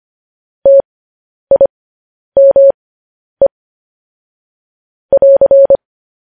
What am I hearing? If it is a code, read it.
Morse code, heard as a keyed tone, about 25 wpm, spaced as for 10 wpm, 550 Hz. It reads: TIME <AR>
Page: News Headlines in Morse Code at 10 WPM